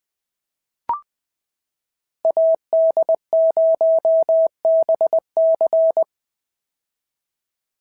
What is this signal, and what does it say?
Morse code, heard as a keyed tone, about 20 wpm, 650 Hz. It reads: AD0BC
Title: POTA QSO Practice - 20wpm